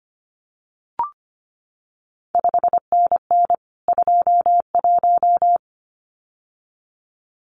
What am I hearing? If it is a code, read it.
Morse code, heard as a keyed tone, about 25 wpm, 700 Hz. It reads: 5NN 21
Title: CQ WW DX Contest Exchange Encore - 25wpm